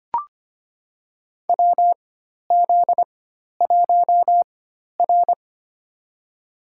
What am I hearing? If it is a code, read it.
Morse code, heard as a keyed tone, about 25 wpm, spaced as for 12 wpm, 700 Hz. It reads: WZ1R
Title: USA Call Signs - 4 Characters - 4x Character Spacing - 25wpm